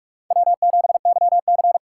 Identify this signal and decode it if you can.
Morse code, heard as a keyed tone, about 46 wpm, 700 Hz. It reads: W7YC
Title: USA Call Signs - 4 Characters - 2x Character Spacing - 45wpm